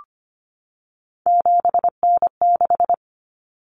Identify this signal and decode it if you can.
Morse code, heard as a keyed tone, about 25 wpm, 700 Hz. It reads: 7N6
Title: Common Cut Numbers - 3-Digit 25wpm